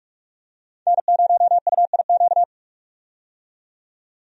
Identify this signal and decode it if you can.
Morse code, heard as a keyed tone, about 43 wpm, 700 Hz. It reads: N0UIQ
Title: USA Call Signs - 5 Characters - 45wpm